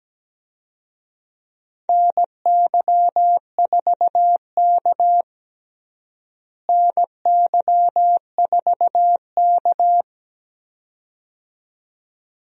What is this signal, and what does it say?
Morse code, heard as a keyed tone, about 17 wpm, 700 Hz. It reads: NY4K NY4K
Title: USA Call Signs - 4 Characters - Repeated 2x - 17wpm